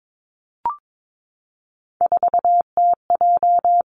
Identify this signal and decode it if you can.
Morse code, heard as a keyed tone, about 22 wpm, 700 Hz. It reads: 4TJ
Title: Three Challenging Letter-Number Combinations 22wpm